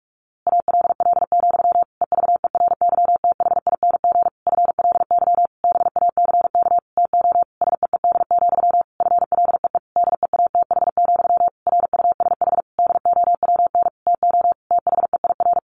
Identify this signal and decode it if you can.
Morse code, heard as a keyed tone, about 46 wpm, 700 Hz. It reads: ALL, EVERYTHING FLY BACK TO SEED, FLEE DEATH, RUSH DOWN TO THEIR
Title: A Sound Of Thunder 45wpm